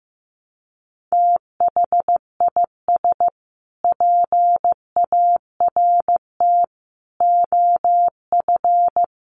Morse code, heard as a keyed tone, about 15 wpm, 700 Hz. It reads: THIS PART OF
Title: Sets of 5 Words Encore2 15wpm